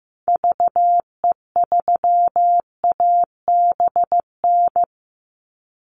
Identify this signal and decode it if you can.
Morse code, heard as a keyed tone, about 15 wpm, 700 Hz. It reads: VE3ABN
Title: Call Signs Repeated 2x - 15wpm